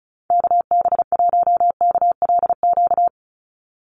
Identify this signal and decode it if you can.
Morse code, heard as a keyed tone, about 34 wpm, 700 Hz. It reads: KB1KLQ
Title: USA Call Signs 35wpm